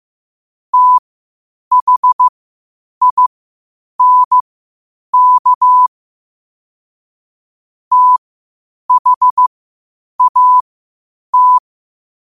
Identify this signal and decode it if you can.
Morse code, heard as a keyed tone, about 15 wpm, spaced as for 8 wpm, 1000 Hz. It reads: THINK THAT
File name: qotdfiles / 8wpm